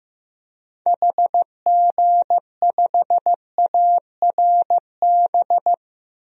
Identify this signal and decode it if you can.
Morse code, heard as a keyed tone, about 15 wpm, 700 Hz. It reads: HG5ARB
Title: DXCC-100 Call Signs - 6 Characters - Repeated 2x - 15wpm